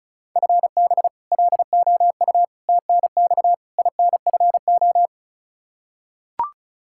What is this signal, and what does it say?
Morse code, heard as a keyed tone, about 34 wpm, 700 Hz. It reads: FB LOU TNX INFO
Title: K1USN SST QSO Practice 35wpm